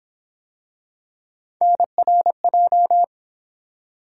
Morse code, heard as a keyed tone, about 29 wpm, 700 Hz. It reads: NRJ